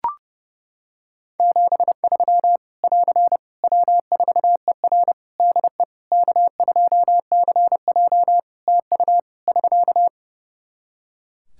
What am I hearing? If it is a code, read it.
Morse code, heard as a keyed tone, about 30 wpm, 700 Hz. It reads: 73 <AR> W4ER DE K2CJ TU <SK>
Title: Practice QSO - Exchange 4 - Ending - 30wpm